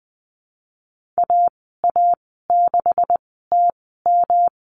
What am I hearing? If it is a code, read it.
Morse code, heard as a keyed tone, about 20 wpm, spaced as for 15 wpm, 700 Hz. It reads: AA6TM
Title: 186 Lesson 30g - Letter b - Call Signs 20wpm